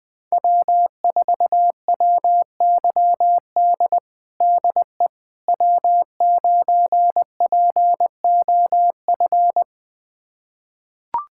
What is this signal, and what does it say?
Morse code, heard as a keyed tone, about 20 wpm, 700 Hz. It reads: W4WYD DE W9POF
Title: USA Callsign DE Callsign - 5 Characters - 20wpm